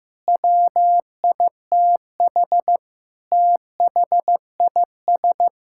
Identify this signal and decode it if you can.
Morse code, heard as a keyed tone, about 15 wpm, 700 Hz. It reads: WITH THIS
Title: Sentences from Top 1000 Words - Review Part 5 - 15wpm